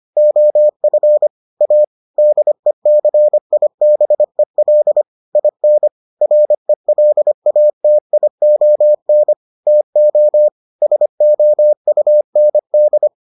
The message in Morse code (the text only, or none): OF A DECIBEL IN RELATION TO SOUND